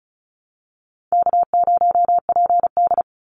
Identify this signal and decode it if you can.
Morse code, heard as a keyed tone, about 34 wpm, 700 Hz. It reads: K0PD
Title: Call Signs - 35wpm